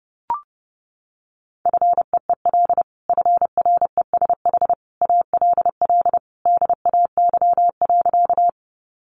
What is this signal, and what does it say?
Morse code, heard as a keyed tone, about 30 wpm, 700 Hz. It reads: FEEL FRESH ALL DAY.
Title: Sentences from Top 1200 Words 30wpm